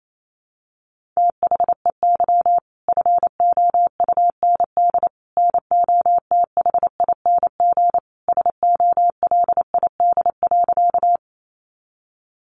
Morse code, heard as a keyed tone, about 28 wpm, 700 Hz. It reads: THEY FOUND NOTHING SOLID.
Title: Sentences from Top 1500 Words 28wpm